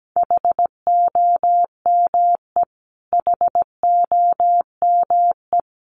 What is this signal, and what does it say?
Morse code, heard as a keyed tone, about 17 wpm, 700 Hz. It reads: HOME HOME
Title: Top 500 Words Repeated 17wpm